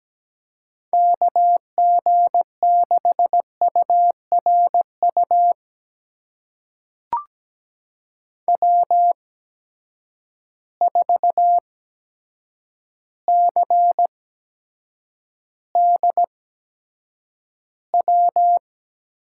Morse code, heard as a keyed tone, about 17 wpm, spaced as for 8 wpm, 700 Hz. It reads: KG6URU W4CDW
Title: USA Call Signs - 8x Character Spacing - 17wpm